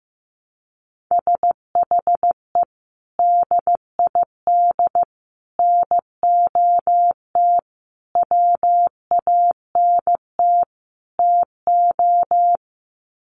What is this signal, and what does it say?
Morse code, heard as a keyed tone, about 15 wpm, 700 Hz. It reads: SHE DID NOT WANT TO
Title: Sets of 5 Words 15wpm